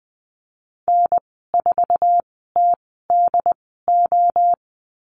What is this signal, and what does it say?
Morse code, heard as a keyed tone, about 20 wpm, spaced as for 15 wpm, 700 Hz. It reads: N4TDO
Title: 069 Lesson 12g - Letter d - Call Signs 20wpm